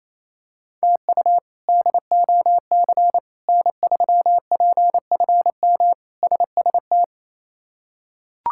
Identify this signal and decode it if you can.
Morse code, heard as a keyed tone, about 28 wpm, 700 Hz. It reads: TU DOC N3PFM SST
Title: K1USN SST QSO Practice - 28wpm